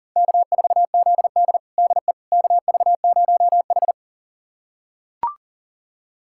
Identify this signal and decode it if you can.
Morse code, heard as a keyed tone, about 40 wpm, 700 Hz. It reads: K4ZD DE KV0H
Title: USA Callsign DE Callsign - 4 Characters - 40wpm